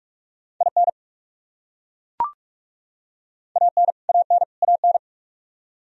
Morse code, heard as a keyed tone, about 41 wpm, 700 Hz. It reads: IN AN AN AN
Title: Top 100 Words Repeated 45wpm